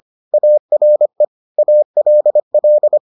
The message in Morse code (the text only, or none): ARE ALL